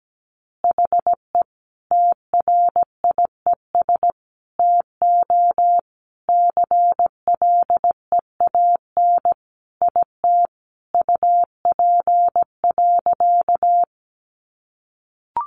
Morse code, heard as a keyed tone, about 17 wpm, 700 Hz. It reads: HE TRIES TO CLEAN IT UP.
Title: Additional Easy Phrases 17wpm